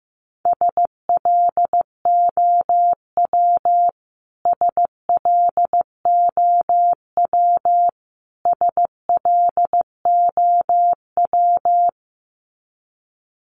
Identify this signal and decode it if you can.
Morse code, heard as a keyed tone, about 15 wpm, 700 Hz. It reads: SLOW SLOW SLOW
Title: Top 500 Words Repeated 15wpm